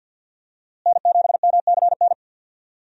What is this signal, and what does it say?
Morse code, heard as a keyed tone, about 52 wpm, 700 Hz. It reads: N7MCN
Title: USA Call Signs - 5 Characters - Repeated 2x - 50wpm